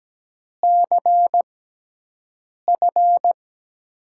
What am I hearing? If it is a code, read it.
Morse code, heard as a keyed tone, about 17 wpm, spaced as for 6 wpm, 700 Hz. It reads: CF
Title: USA Call Signs - 6x Character Spacing - 17wpm